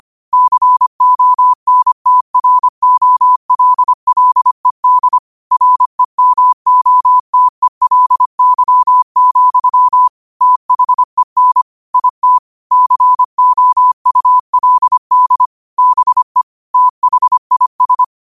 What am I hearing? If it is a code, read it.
Morse code, heard as a keyed tone, about 25 wpm, 1000 Hz. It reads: CONTROLLED REMOTELY, THEN IT COULD BE THIS
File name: qotdfiles / 25wpm